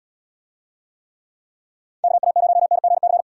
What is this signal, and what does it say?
Morse code, heard as a keyed tone, about 75 wpm, 700 Hz. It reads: KA0NGC